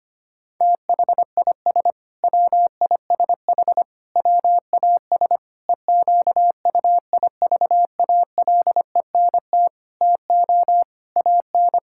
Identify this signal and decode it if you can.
Morse code, heard as a keyed tone, about 25 wpm, 700 Hz. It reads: THIS WISH WAS EQUIVALENT TO AN
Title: Sentences from Top 1600 Words 25wpm